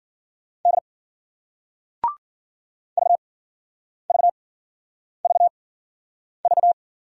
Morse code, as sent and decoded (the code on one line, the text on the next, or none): -.. / ...- ...- ...- ...-
D VVVV